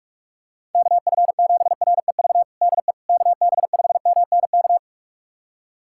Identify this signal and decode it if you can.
Morse code, heard as a keyed tone, about 46 wpm, 700 Hz. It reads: KF7REV DE KB5MNK
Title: USA Callsign DE Callsign - 6 Characters - 45wpm